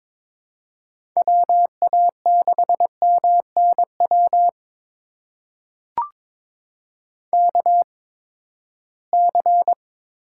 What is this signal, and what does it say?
Morse code, heard as a keyed tone, about 21 wpm, spaced as for 15 wpm, 700 Hz. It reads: WA6MNW KC
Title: USA Call Signs - 6 Characters - 8x Character Spacing - 22wpm